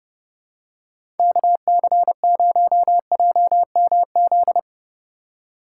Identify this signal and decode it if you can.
Morse code, heard as a keyed tone, about 30 wpm, 700 Hz. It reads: KC0JMZ
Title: USA Call Signs - 6 Characters - 4x Character Spacing - 30wpm